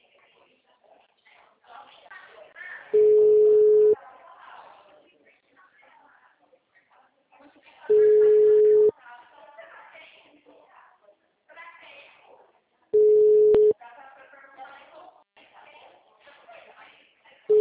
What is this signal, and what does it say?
Ringback tone (Europe, 425 Hz), with speech.
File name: out- -2019- - - .11585